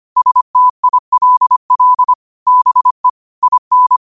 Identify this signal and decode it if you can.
Morse code, heard as a keyed tone, about 25 wpm, 1000 Hz. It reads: STILL BE IN
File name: qotdfiles / 25wpm